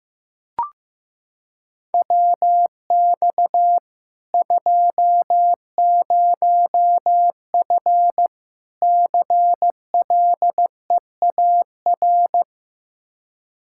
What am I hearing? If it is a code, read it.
Morse code, heard as a keyed tone, about 15 wpm, 700 Hz. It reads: WX 20F CLEAR
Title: K1USN SST QSO Practice 15wpm